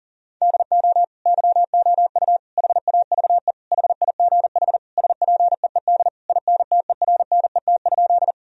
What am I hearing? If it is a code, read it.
Morse code, heard as a keyed tone, about 40 wpm, 700 Hz. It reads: DO YOU HAVE HIGH SPEED INTERNET?
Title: Sentences from Top 1000 Words - Review Part 4 - 40wpm